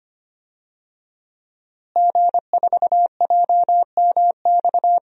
Morse code, heard as a keyed tone, about 25 wpm, 700 Hz. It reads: G4JMX
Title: DXCC-100 Call Signs - 5 Characters - Repeated 3x - 25wpm